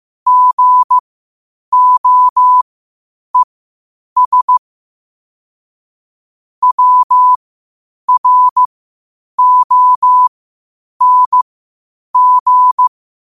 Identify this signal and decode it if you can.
Morse code, heard as a keyed tone, about 15 wpm, spaced as for 8 wpm, 1000 Hz. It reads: GOES WRONG
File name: qotdfiles / 8wpm